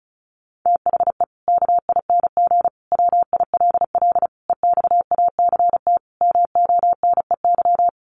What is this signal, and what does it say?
Morse code, heard as a keyed tone, about 34 wpm, 700 Hz. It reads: THE KING WILL EXACT MONEY
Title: Sentences from Top 1300 Words 35wpm